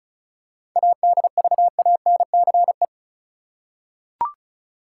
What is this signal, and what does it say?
Morse code, heard as a keyed tone, about 34 wpm, 700 Hz. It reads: ADVANCE
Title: Seven Letter Words - 35wpm